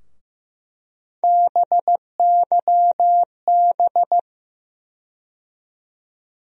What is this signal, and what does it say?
Morse code, heard as a keyed tone, about 15 wpm, 700 Hz. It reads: BYB
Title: USA Call Sign Postfixes - 3 Letters 15wpm